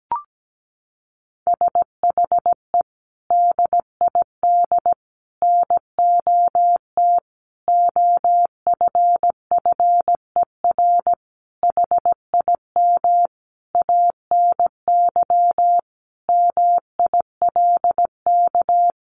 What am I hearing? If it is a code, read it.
Morse code, heard as a keyed tone, about 17 wpm, 700 Hz. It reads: SHE DID NOT OFFER HIM ANY MILK
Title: Sentences from Top 1500 Words 17wpm